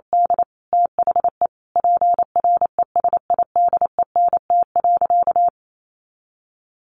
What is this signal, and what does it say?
Morse code, heard as a keyed tone, about 28 wpm, 700 Hz. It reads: D THE PRESIDENT.
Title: Sentences from Top 1000 Words 28wpm